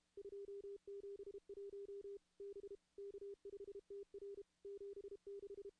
Morse code, heard as a keyed tone, about 33 wpm, 400 Hz. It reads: 271 B K5TR 76
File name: contests / CW-SS-k5tr- -2300